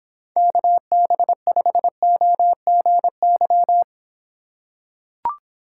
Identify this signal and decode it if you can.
Morse code, heard as a keyed tone, about 26 wpm, 700 Hz. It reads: KB5OGY